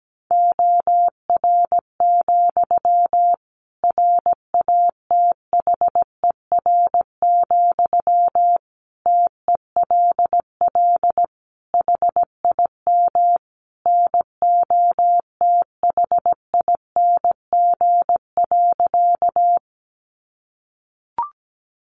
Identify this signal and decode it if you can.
Morse code, heard as a keyed tone, about 17 wpm, 700 Hz. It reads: OR, RATHER, TELL HIM NOTHING.
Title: Sentences from Top 700 Words 17wpm